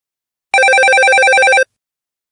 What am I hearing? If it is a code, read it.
There is no code.